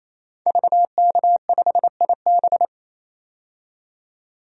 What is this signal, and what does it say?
Morse code, heard as a keyed tone, about 28 wpm, 700 Hz. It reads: VK5IB
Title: DXCC-100 Call Signs - 5 Characters - 28wpm